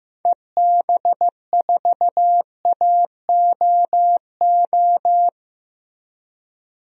Morse code, heard as a keyed tone, about 15 wpm, 700 Hz. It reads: EB4AOO
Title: DXCC-100 Call Signs - 6 Characters - Repeated 3x - 15wpm